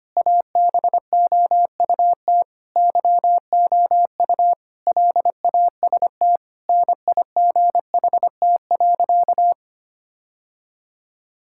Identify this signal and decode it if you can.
Morse code, heard as a keyed tone, about 25 wpm, 700 Hz. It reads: ABOUT YOU LAST NIGHT.
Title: Sentences from Top 1600 Words 25wpm